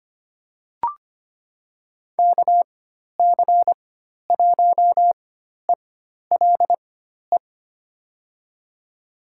Morse code, heard as a keyed tone, about 25 wpm, spaced as for 12 wpm, 700 Hz. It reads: KC1ELE
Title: USA Call Signs - 4x Character Spacing - 25wpm